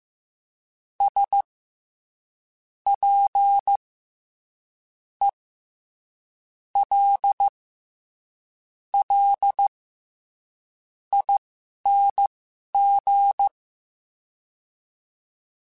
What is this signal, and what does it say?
Morse code, heard as a keyed tone, about 15 wpm, spaced as for 11 wpm, 800 Hz. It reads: S P E L L ING
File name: Special INGs-mp3s / Special INGs-mp3s / SlowToFaster ing @10